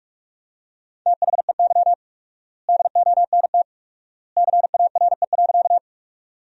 Morse code, heard as a keyed tone, about 46 wpm, 700 Hz. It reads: THEY DONT CARE.
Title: 4x Word-Spacing - Sentences from Top 300 Words - 45wpm